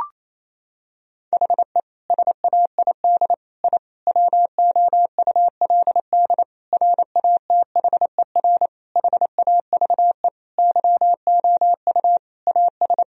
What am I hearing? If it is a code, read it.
Morse code, heard as a keyed tone, about 28 wpm, 700 Hz. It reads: HE SAID I WOULD RATHER HAVE YOU AS
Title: Sentences from Top 1000 Words - Review Part 4 - 28wpm